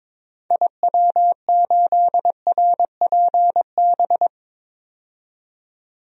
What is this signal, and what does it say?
Morse code, heard as a keyed tone, about 22 wpm, 700 Hz. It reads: IW8RPB
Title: DXCC-100 Call Signs - 6 Characters - 22wpm